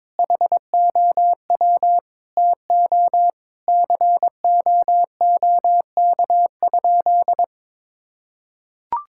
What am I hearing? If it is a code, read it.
Morse code, heard as a keyed tone, about 22 wpm, 700 Hz. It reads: HOW TO COOK?
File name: Sentences with 7-Word Phrases - 22wpm